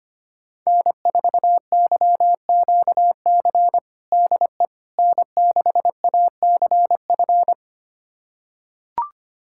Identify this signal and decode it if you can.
Morse code, heard as a keyed tone, about 25 wpm, 700 Hz. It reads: N4YQC DE N6ACF
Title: USA Callsign DE Callsign - 5 Characters - 25wpm